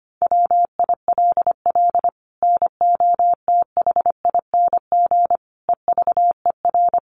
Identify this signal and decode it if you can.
Morse code, heard as a keyed tone, about 25 wpm, 700 Hz. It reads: WILL NOTHING EVER
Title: Sentences from Top 1200 Words 25wpm